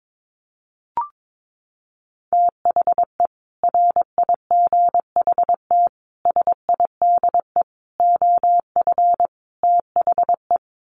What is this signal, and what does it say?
Morse code, heard as a keyed tone, about 22 wpm, 700 Hz. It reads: THE RIGHT SIDE OF THE
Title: Sets of 5 Words Encore 22wpm